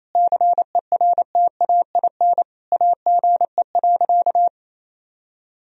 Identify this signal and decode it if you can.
Morse code, heard as a keyed tone, about 28 wpm, 700 Hz. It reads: CERTAIN AGE.
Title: Sentences from Top 1000 Words - Review Part 3 - 28wpm